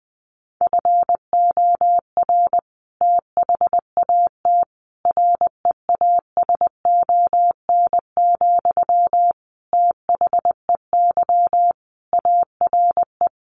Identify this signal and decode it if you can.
Morse code, heard as a keyed tone, about 20 wpm, 700 Hz. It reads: FOR THAT REASON, THEY ARE
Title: Sentences from Top 1500 Words 20wpm